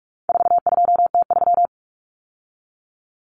Morse code, heard as a keyed tone, about 44 wpm, 700 Hz. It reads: 42T3
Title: Common Cut Numbers - 4-Digit 45wpm